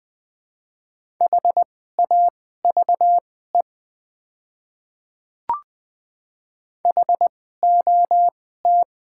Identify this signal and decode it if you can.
Morse code, heard as a keyed tone, about 20 wpm, spaced as for 14 wpm, 700 Hz. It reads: HAVE HOT
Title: 2x Character Spacing - Top 100 Words - 20wpm